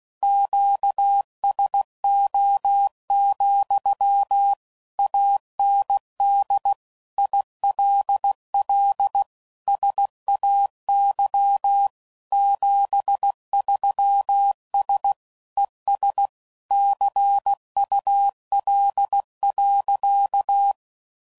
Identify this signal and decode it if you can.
Morse code, heard as a keyed tone, about 16 wpm, 800 Hz. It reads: QSO, AND ILL SAY 73S ES CUL.
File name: CW Code Groups @15WPM / QSOs / WA7QOX QRX